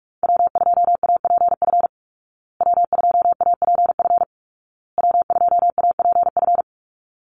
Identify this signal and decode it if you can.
Morse code, heard as a keyed tone, about 44 wpm, 700 Hz. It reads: W2APF W2APF W2APF
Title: Call Signs Repeated 2x - 45wpm